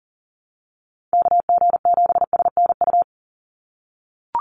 Morse code, heard as a keyed tone, about 40 wpm, 700 Hz. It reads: KG7SNU
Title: USA Call Signs 40wpm